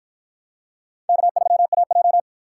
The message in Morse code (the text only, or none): K3AJ